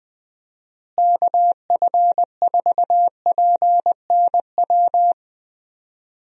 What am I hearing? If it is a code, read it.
Morse code, heard as a keyed tone, about 21 wpm, 700 Hz. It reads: KF4PNW
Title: USA Call Signs - 6 Characters - Repeated 2x - 20wpm